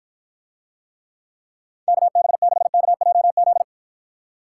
Morse code, heard as a keyed tone, about 52 wpm, 700 Hz. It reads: KB6KJZ